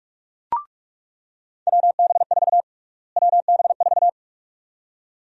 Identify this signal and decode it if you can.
Morse code, heard as a keyed tone, about 46 wpm, 700 Hz. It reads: WB4 WB4
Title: USA Call Sign Prefixes - Repeated 2x - 45wpm